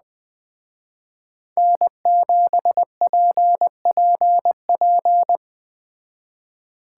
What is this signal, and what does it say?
Morse code, heard as a keyed tone, about 20 wpm, 700 Hz. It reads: N7PPP